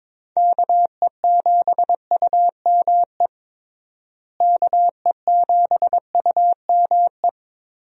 Morse code, heard as a keyed tone, about 22 wpm, 700 Hz. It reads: KE7UME KE7UME
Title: USA Call Signs - 6 Characters - Repeated 2x - 22wpm